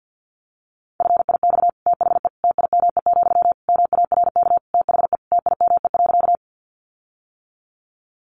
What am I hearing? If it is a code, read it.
Morse code, heard as a keyed tone, about 50 wpm, 700 Hz. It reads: FIX THE TIME, MARK THE TIME.
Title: Sentences from Top 1400 Words 50wpm